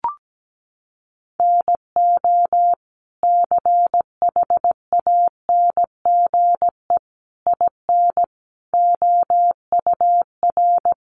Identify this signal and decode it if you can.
Morse code, heard as a keyed tone, about 17 wpm, 700 Hz. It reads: NO CHANGE IN OUR
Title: Sentences from Top 1200 Words 17wpm